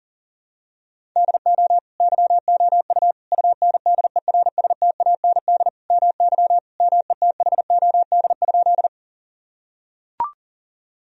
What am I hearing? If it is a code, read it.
Morse code, heard as a keyed tone, about 37 wpm, 700 Hz. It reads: DO YOU UNDERSTAND MY METHOD?
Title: Sentences from Top 1000 Words - Review Part 3 - 40wpm